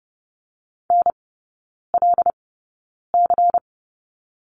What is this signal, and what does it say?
Morse code, heard as a keyed tone, about 30 wpm, spaced as for 9 wpm, 700 Hz. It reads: NLC
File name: Lesson 17e - Letter u - Three Characters 30wpm